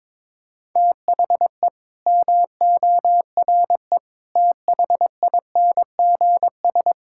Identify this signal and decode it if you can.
Morse code, heard as a keyed tone, about 22 wpm, 700 Hz. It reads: THE MORE THINGS
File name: Sentences with 9-Word Phrases - 22wpm